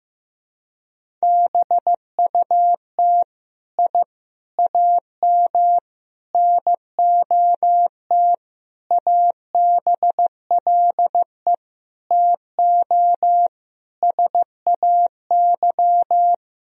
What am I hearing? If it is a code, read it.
Morse code, heard as a keyed tone, about 15 wpm, 700 Hz. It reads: BUT I AM NOT ABLE TO SAY
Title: Sentences from Top 1000 Words - Review Part 2 - 15wpm